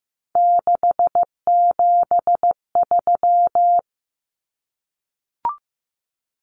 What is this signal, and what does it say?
Morse code, heard as a keyed tone, about 15 wpm, 700 Hz. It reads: 673
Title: Common Cut Numbers - 3-Digit 15wpm